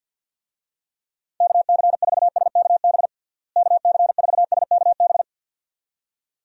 Morse code, heard as a keyed tone, about 51 wpm, 700 Hz. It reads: KC4SKB KC4SKB
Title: USA Call Signs - 6 Characters - Repeated 2x - 50wpm